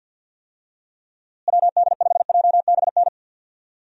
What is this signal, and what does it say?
Morse code, heard as a keyed tone, about 50 wpm, 700 Hz. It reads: WD5JBN
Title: USA Call Signs - 6 Characters - Repeated 2x - 50wpm